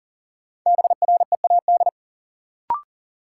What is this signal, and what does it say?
Morse code, heard as a keyed tone, about 40 wpm, 700 Hz. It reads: BREAD
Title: Sentences from Top 1600 Words - New Word Drill Alt - 40wpm